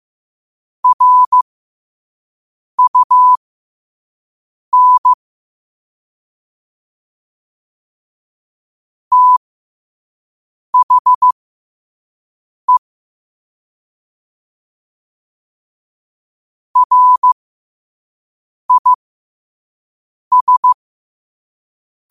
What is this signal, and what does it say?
Morse code, heard as a keyed tone, about 15 wpm, spaced as for 5 wpm, 1000 Hz. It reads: RUN THE RIS